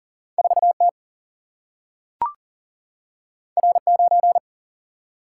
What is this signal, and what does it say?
Morse code, heard as a keyed tone, about 39 wpm, 700 Hz. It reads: 4T R9
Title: Single-Letter Single-Number 40wpm